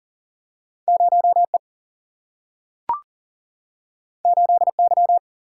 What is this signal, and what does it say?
Morse code, heard as a keyed tone, about 40 wpm, 700 Hz. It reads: 0E 8Y
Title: Single-Letter Single-Number 40wpm